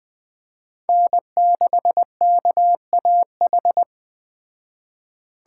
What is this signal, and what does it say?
Morse code, heard as a keyed tone, about 20 wpm, 700 Hz. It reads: N6KAH